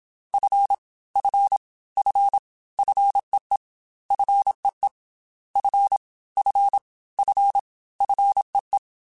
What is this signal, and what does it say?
Morse code, heard as a keyed tone, about 27 wpm, 800 Hz. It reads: F F F FEE FEE F F F FEE
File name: F and L / F and L mp3 / F and L Prefixes 3x2 Twice @25